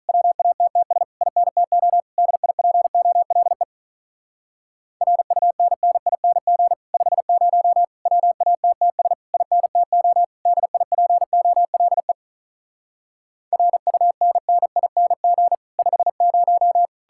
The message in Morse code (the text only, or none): WATTS INTO DIPOLE RUNNING 50 WATTS INTO DIPOLE RUNNING 50